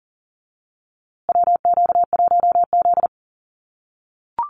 Morse code, heard as a keyed tone, about 40 wpm, 700 Hz. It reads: WQ1Z
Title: USA Call Signs Repeated 2x 40wpm